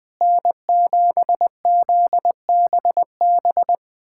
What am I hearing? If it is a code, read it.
Morse code, heard as a keyed tone, about 21 wpm, 700 Hz. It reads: N7ZBB
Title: USA Callsign DE Callsign - 5 Characters - 20wpm